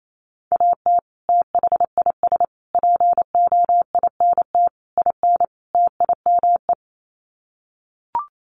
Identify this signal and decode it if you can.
Morse code, heard as a keyed tone, about 28 wpm, 700 Hz. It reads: AT THIS POINT IN TIME
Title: Sets of 5 Words Encore 28wpm